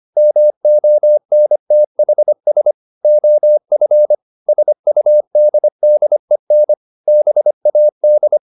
Morse code, heard as a keyed tone, about 25 wpm, 600 Hz. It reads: MONTHS OF SUDDEN BAD
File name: the old man and the sea part 1 25wpm